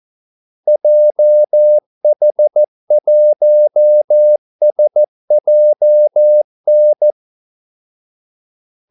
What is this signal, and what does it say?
Morse code, heard as a keyed tone, about 14 wpm, 600 Hz. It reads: JH1SJN
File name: Callsigns-14wpm